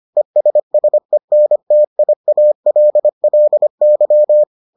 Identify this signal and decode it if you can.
Morse code, heard as a keyed tone, about 25 wpm, 600 Hz. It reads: ESSENTIALLY